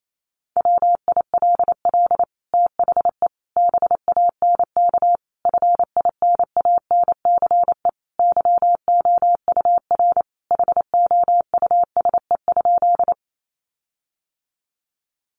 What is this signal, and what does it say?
Morse code, heard as a keyed tone, about 28 wpm, 700 Hz. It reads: WILL THE BANK FINANCE YOUR HOUSE?
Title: Sentences from Top 1500 Words 28wpm